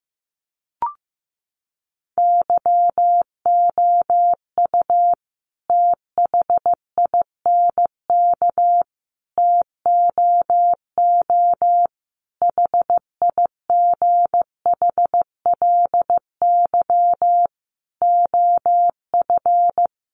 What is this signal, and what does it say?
Morse code, heard as a keyed tone, about 15 wpm, 700 Hz. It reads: YOU THINK TOO HIGHLY OF
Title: Sentences from Top 1100 Words 15wpm